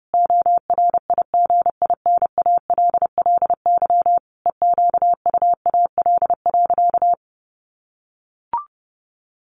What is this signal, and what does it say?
Morse code, heard as a keyed tone, about 30 wpm, 700 Hz. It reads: ORIGINALLY EQUAL.
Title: Sentences from Top 1500 Words 30wpm